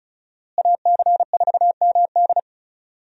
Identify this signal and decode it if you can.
Morse code, heard as a keyed tone, about 36 wpm, 700 Hz. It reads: AC4MD
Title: USA Call Signs - 5 Characters - Repeated 3x - 35wpm